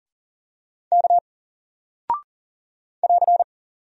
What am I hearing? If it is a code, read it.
Morse code, heard as a keyed tone, about 40 wpm, 700 Hz. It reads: K <AR>
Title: Single Character (Letter:Number:Punctuation:Prosign) Alt 40wpm